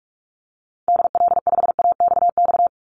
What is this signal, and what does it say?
Morse code, heard as a keyed tone, about 44 wpm, 700 Hz. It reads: DL5AXX
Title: Call Signs - 45wpm